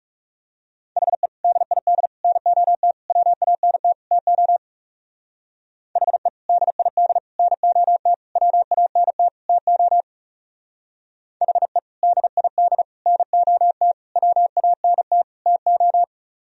Morse code, heard as a keyed tone, about 37 wpm, 700 Hz. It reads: HE DID NOT WANT TO HE DID NOT WANT TO HE DID NOT WANT TO
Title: Speed-Racing - Sets of 5 Words 30wpm